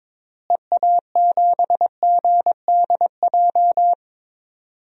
Morse code, heard as a keyed tone, about 22 wpm, 700 Hz. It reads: EA7GDJ
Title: DXCC-100 Call Signs - 6 Characters - Repeated 3x - 22wpm